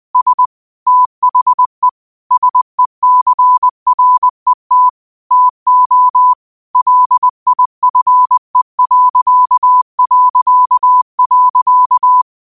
Morse code, heard as a keyed tone, about 20 wpm, 1000 Hz. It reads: S THE SECRET TO LIFE...